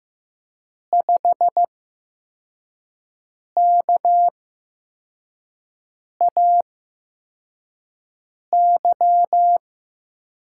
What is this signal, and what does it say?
Morse code, heard as a keyed tone, about 15 wpm, spaced as for 4 wpm, 700 Hz. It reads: 5KAY
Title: USA Call Signs - 5 Characters - 8x Character Spacing - 15wpm